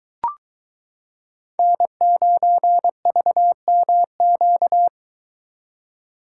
Morse code, heard as a keyed tone, about 21 wpm, 700 Hz. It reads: N9VMQ